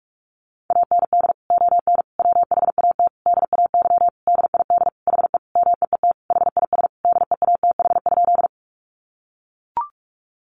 Morse code, heard as a keyed tone, about 44 wpm, 700 Hz. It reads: AND ON WHAT DAY DID HE MEET HIS DEATH?
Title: Sentences from Top 1100 Words 45wpm